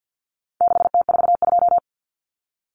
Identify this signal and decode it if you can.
Morse code, heard as a keyed tone, about 50 wpm, 700 Hz. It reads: 6T42
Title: Common Cut Numbers - 4-Digit 50wpm